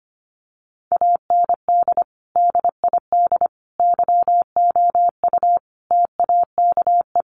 Morse code, heard as a keyed tone, about 25 wpm, 700 Hz. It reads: AND DID YOU TAKE
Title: Sentences from Top 1500 Words 25wpm